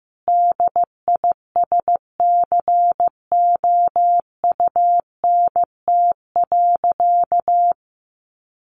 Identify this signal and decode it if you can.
Morse code, heard as a keyed tone, about 15 wpm, 700 Hz. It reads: DISCOUNT.
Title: Sentences from Top 1200 Words 15wpm